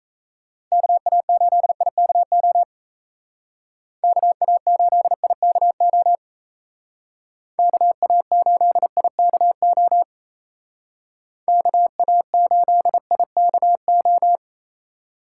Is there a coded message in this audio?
Yes: Morse code from 0.7 to 14.4 s, heard as a keyed tone, about 36 wpm, 700 Hz. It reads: KA8IKO KA8IKO KA8IKO KA8IKO